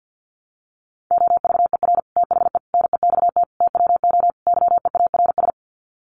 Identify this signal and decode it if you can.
Morse code, heard as a keyed tone, about 50 wpm, 700 Hz. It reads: OVER THE NEXT TWO YEARS
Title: Sets of 5 Words Encore2 50wpm